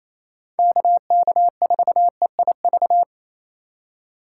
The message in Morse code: KK4EIV

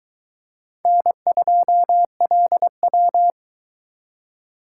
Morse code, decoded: N2LW

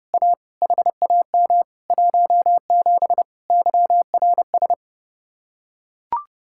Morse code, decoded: A HAM 17 YRS

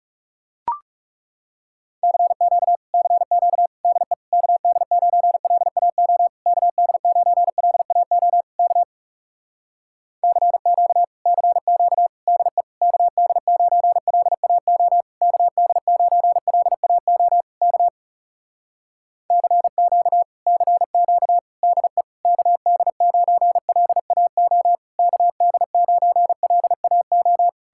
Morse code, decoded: CQ CQ DE KD9LAO KD9LAO K CQ CQ DE KD9LAO KD9LAO K CQ CQ DE KD9LAO KD9LAO

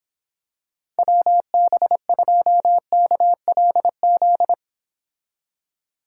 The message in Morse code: WB2KLZ

